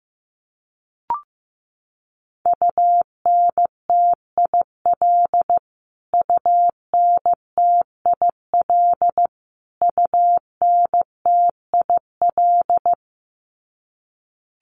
Morse code, heard as a keyed tone, about 15 wpm, 700 Hz. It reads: UNTIL UNTIL UNTIL
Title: Top 500 Words Repeated 15wpm